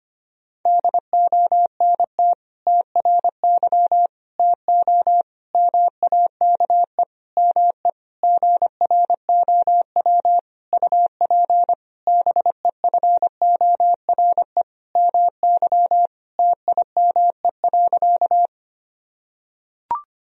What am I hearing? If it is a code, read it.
Morse code, heard as a keyed tone, about 25 wpm, 700 Hz. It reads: DONT TRY TO MAKE ME GROW UP BEFORE MY TIME.
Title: Sentences from Top 1000 Words - Review Part 1 - 25wpm